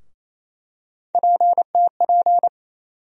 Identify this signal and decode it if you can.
Morse code, heard as a keyed tone, about 28 wpm, 700 Hz. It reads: PTP
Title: USA Call Sign Postfixes - 3 Letters 28wpm